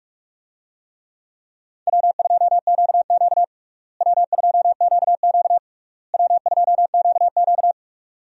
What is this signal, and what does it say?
Morse code, heard as a keyed tone, about 41 wpm, 700 Hz. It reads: W2QQ W2QQ W2QQ
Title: USA Call Signs - 4 Characters - Repeated 3x - 45wpm